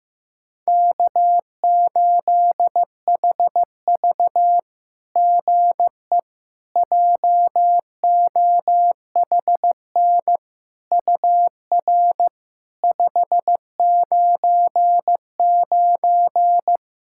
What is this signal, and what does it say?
Morse code, heard as a keyed tone, about 15 wpm, 700 Hz. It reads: K8HV GE JOHN UR 599
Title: SOTA QSO Practice 15wpm